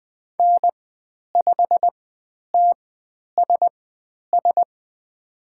Morse code, heard as a keyed tone, about 20 wpm, spaced as for 10 wpm, 700 Hz. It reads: N5TSS
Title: 095 Lesson 16g - Letter c - Call Signs 20wpm